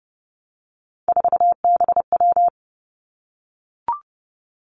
Morse code, heard as a keyed tone, about 30 wpm, 700 Hz. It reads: 4BW
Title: Three Challenging Letter-Number Combinations 30wpm